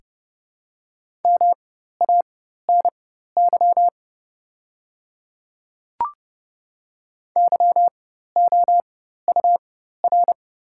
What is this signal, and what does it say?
Morse code, heard as a keyed tone, about 30 wpm, spaced as for 14 wpm, 700 Hz. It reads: MANY YOUR
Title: 4x Character Spacing - Top 100 Words - 30wpm